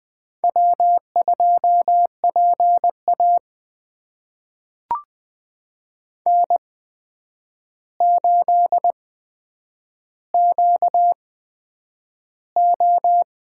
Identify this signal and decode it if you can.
Morse code, heard as a keyed tone, about 20 wpm, spaced as for 9 wpm, 700 Hz. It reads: W2PA N8QO
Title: USA Call Signs - 4 Characters - 8x Character Spacing - 20wpm